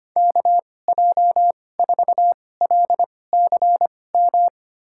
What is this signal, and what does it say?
Morse code, heard as a keyed tone, about 25 wpm, spaced as for 18 wpm, 700 Hz. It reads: KJ4LCM
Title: USA Call Signs - 6 Characters - 2x Character Spacing - 25wpm